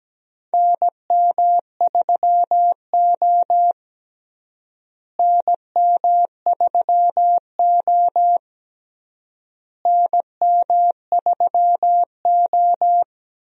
Morse code, heard as a keyed tone, about 17 wpm, 700 Hz. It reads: NM3O NM3O NM3O
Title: USA Call Signs - 4 Characters - Repeated 3x - 17wpm